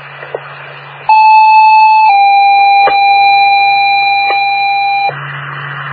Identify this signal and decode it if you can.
Two-tone page, heard as a one-second tone, then a three-second tone, about 870 then 800 Hz, with noise. It is heard from 1.1 to 5.1 s.